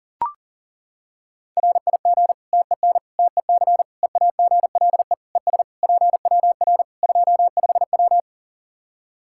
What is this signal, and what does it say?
Morse code, heard as a keyed tone, about 40 wpm, 700 Hz. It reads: RIG TEN TEC EAGLE ES PWR 25W
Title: K1USN SST QSO Practice 40wpm